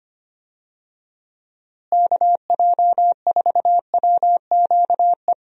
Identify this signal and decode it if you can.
Morse code, heard as a keyed tone, about 25 wpm, 700 Hz. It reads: KJ4WQE
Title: USA Call Signs - 6 Characters - Repeated 2x - 25wpm